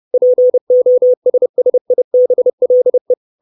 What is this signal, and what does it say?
Morse code, heard as a keyed tone, about 30 wpm, 500 Hz. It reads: POSSIBLE
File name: 632 Words 30WPM